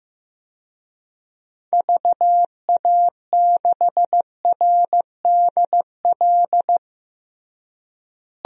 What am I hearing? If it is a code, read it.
Morse code, heard as a keyed tone, about 15 wpm, 700 Hz. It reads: VA6RDL
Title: DXCC-100 Call Signs - 6 Characters - Repeated 3x - 15wpm